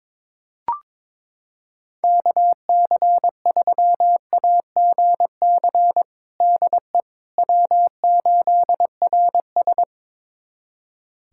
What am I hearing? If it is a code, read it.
Morse code, heard as a keyed tone, about 22 wpm, 700 Hz. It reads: KC3AGC DE W8RS
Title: USA Callsign DE Callsign 22wpm